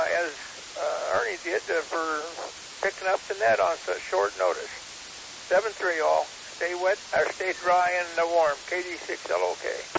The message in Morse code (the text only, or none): none